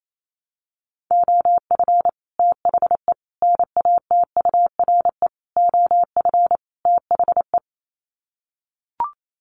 Morse code, heard as a keyed tone, about 28 wpm, 700 Hz. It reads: OF THE NATURE OF THE
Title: Sets of 5 Words Encore2 28wpm